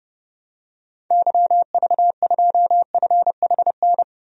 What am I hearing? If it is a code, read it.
Morse code, heard as a keyed tone, about 30 wpm, 700 Hz. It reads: YV2FHN
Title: DXCC-100 Call Signs - 6 Characters - Repeated 3x - 30wpm